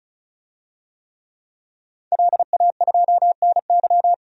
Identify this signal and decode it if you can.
Morse code, heard as a keyed tone, about 36 wpm, 700 Hz. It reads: LA2NY
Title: DXCC-100 Call Signs - 5 Characters - 35wpm